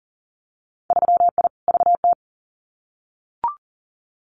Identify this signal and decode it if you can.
Morse code, heard as a keyed tone, about 41 wpm, 700 Hz. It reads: 3I VT
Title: Winter Field Day Exchanges Encore 40wpm